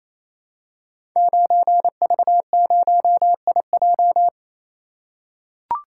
Morse code, heard as a keyed tone, about 27 wpm, 700 Hz. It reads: 9V0IJ
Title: Code Groups - 5 Letter:Numbers - 28wpm